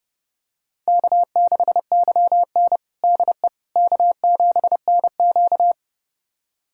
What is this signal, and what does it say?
Morse code, heard as a keyed tone, about 30 wpm, 700 Hz. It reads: K6YN DE K7NQ
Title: USA Callsign DE Callsign - 4 Characters - 30wpm